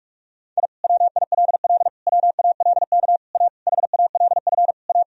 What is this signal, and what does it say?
Morse code, heard as a keyed tone, about 47 wpm, 700 Hz. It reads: I WILL WALK A HALF A
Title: Sentences from Top 1000 Words - Review Part 1 - 45wpm